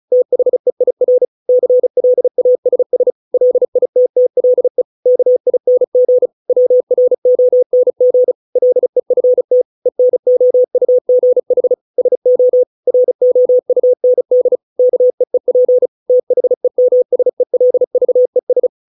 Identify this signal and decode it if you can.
Morse code, heard as a keyed tone, about 34 wpm, 500 Hz. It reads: THEIR CLASS LITTLE KING WRONG LEFT ENOUGH SO ROUND KEEP THEMSELVES